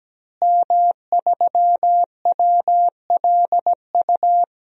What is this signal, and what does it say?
Morse code, heard as a keyed tone, about 17 wpm, 700 Hz. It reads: M3WLU
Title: DXCC-100 Call Signs - 5 Characters - Repeated 3x - 17wpm